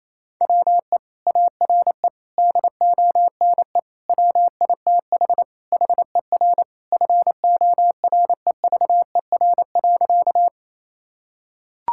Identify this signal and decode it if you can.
Morse code, heard as a keyed tone, about 27 wpm, 700 Hz. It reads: WE ARE DONE WITH HER FOREVER.
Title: Sentences from Top 1600 Words 28wpm